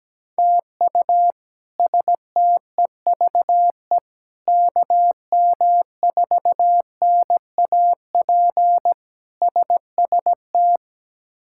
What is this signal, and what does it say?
Morse code, heard as a keyed tone, about 17 wpm, 700 Hz. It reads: TU STEVE KM4NAP SST
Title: K1USN SST QSO Practice - 17wpm